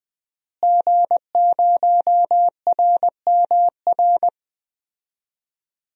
Morse code, heard as a keyed tone, about 20 wpm, 700 Hz. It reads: G0RMR
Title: DXCC-100 Call Signs - 5 Characters - Repeated 3x - 20wpm